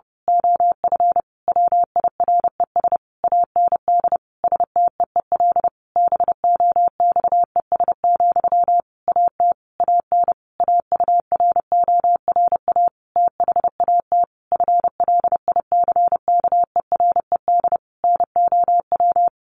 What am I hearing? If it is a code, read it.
Morse code, heard as a keyed tone, about 30 wpm, 700 Hz. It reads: OF WIRES AND STEEL BOXES, AT AN AURORA THAT FLICKERED NOW
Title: A Sound Of Thunder 30wpm